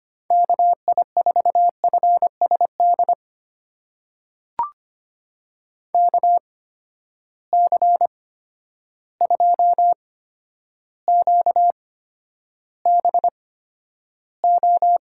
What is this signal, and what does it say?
Morse code, heard as a keyed tone, about 25 wpm, spaced as for 11 wpm, 700 Hz. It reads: KI4FSD KC2QBO
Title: USA Call Signs - 6 Characters - 8x Character Spacing - 25wpm